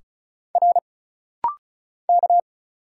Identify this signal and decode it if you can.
Morse code, heard as a keyed tone, about 36 wpm, 700 Hz. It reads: R K
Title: Single Letters - Rapid-Fire 35wpm